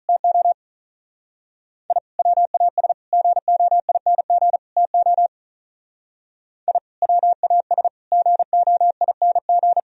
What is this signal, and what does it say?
Morse code, heard as a keyed tone, about 43 wpm, 700 Hz. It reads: TO I WAS GOING TO I WAS GOING
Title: Speed-Racing - Sets of 4 Words 35wpm